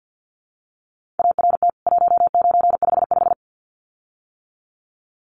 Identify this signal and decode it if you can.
Morse code, heard as a keyed tone, about 48 wpm, 700 Hz. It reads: ART 1955
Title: CWT Exchange 50wpm